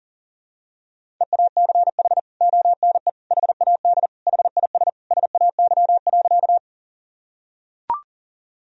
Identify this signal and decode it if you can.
Morse code, heard as a keyed tone, about 40 wpm, 700 Hz. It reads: EACH ONE HAD HIS SAY.
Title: Sentences from Top 1000 Words - Review Part 1 - 40wpm